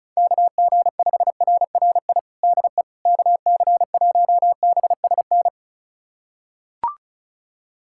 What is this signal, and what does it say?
Morse code, heard as a keyed tone, about 36 wpm, 700 Hz. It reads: KG5RRI DE KC1BSN
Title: USA Callsign DE Callsign 35wpm